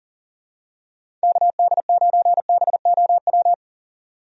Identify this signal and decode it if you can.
Morse code, heard as a keyed tone, about 40 wpm, 700 Hz. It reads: KD9BOW
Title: USA Call Signs - 6 Characters - 40wpm